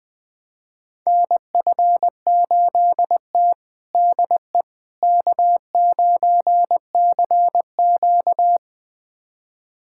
Morse code, heard as a keyed tone, about 20 wpm, 700 Hz. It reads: NF8T DE K9CQ
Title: USA Callsign DE Callsign - 4 Characters - 20wpm